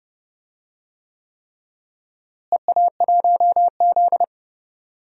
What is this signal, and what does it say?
Morse code, heard as a keyed tone, about 30 wpm, 700 Hz. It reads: EA1Z
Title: DXCC-100 Call Signs - 4 Characters - Repeated 3x - 30wpm